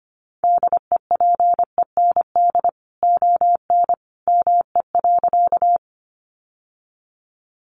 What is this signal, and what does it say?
Morse code, heard as a keyed tone, about 25 wpm, 700 Hz. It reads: DEPEND ON ME.
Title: Sentences from Top 1300 Words 25wpm